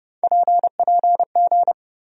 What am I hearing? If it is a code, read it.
Morse code, heard as a keyed tone, about 31 wpm, 700 Hz. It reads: PPG
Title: USA Call Sign Postfixes - 3 Letters 30wpm